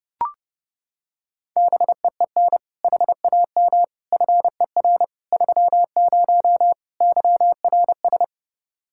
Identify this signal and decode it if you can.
Morse code, heard as a keyed tone, about 29 wpm, 700 Hz. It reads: BEEN HAM FER 30 YRS